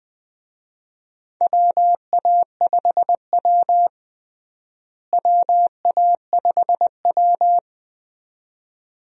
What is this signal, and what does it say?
Morse code, heard as a keyed tone, about 21 wpm, 700 Hz. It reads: WA5W WA5W
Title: USA Call Signs - 4 Characters - Repeated 2x - 20wpm